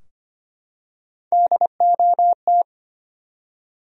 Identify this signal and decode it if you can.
Morse code, heard as a keyed tone, about 25 wpm, 700 Hz. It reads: DOT
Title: USA Common Names - 25wpm